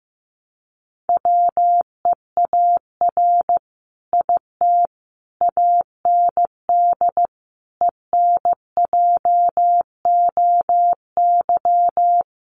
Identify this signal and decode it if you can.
Morse code, heard as a keyed tone, about 15 wpm, 700 Hz. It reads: WEAR IT AND ENJOY
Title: Sentences from Top 1500 Words 15wpm